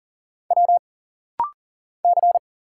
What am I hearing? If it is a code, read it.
Morse code, heard as a keyed tone, about 40 wpm, 700 Hz. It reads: W C
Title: Single Letters - Rapid-Fire 40wpm